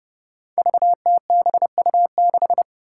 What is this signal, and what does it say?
Morse code, heard as a keyed tone, about 29 wpm, 700 Hz. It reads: VTBU6
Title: Five Challenging Letter-Number Combinations 30wpm